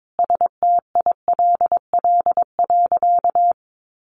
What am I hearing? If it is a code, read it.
Morse code, heard as a keyed tone, about 22 wpm, 700 Hz. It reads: STILL.
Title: Sentences from Top 1000 Words 22wpm